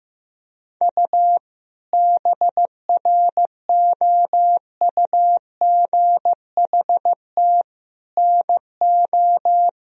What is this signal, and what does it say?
Morse code, heard as a keyed tone, about 15 wpm, 700 Hz. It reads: U BROUGHT NO
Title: Sentences from Top 500 Words and Abbreviations 15wpm